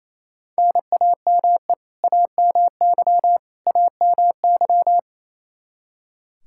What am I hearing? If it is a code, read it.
Morse code, heard as a keyed tone, about 28 wpm, 700 Hz. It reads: NAME AMY AMY
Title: Practice QSO - Exchange 1 - Beginning - 28wpm